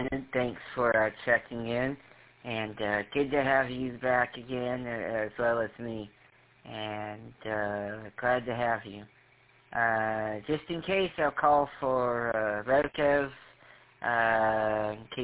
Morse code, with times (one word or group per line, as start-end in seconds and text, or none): none